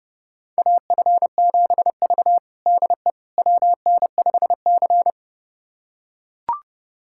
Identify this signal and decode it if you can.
Morse code, heard as a keyed tone, about 32 wpm, 700 Hz. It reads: AF7V DE WN5C
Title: USA Callsign DE Callsign - 4 Characters - 30wpm